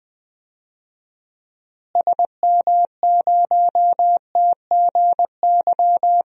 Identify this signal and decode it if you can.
Morse code, heard as a keyed tone, about 20 wpm, 700 Hz. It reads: SM0TGY
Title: DXCC-100 Call Signs - 6 Characters - Repeated 2x - 20wpm